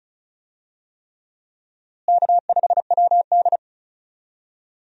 Morse code, heard as a keyed tone, about 36 wpm, 700 Hz. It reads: K5WD
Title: USA Call Signs - 4 Characters - Repeated 3x - 35wpm